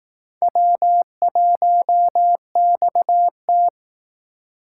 Morse code, heard as a keyed tone, about 18 wpm, 700 Hz. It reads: W1XT
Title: Speed Racing - USA 1x2 Call Signs 15wpm